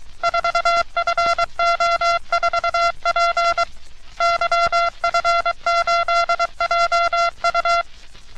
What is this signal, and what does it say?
Morse code, heard as a keyed tone, about 23 wpm, 1370 Hz. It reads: 4FO4P YF8JU
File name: U.S. Navy - RN-2849-124 - Sixteens Morse Code Practice